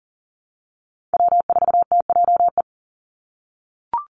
Morse code, heard as a keyed tone, about 39 wpm, 700 Hz. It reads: W4TJE
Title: Call Signs - 40wpm